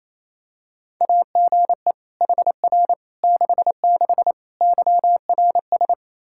Morse code, heard as a keyed tone, about 28 wpm, 700 Hz. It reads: AGE HR 66 YRS
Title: Practice QSO - Exchange 3 - More Info - 28wpm